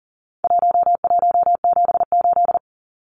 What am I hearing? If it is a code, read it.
Morse code, heard as a keyed tone, about 40 wpm, 700 Hz. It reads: 1178
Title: Numbers - 4-Digit 40wpm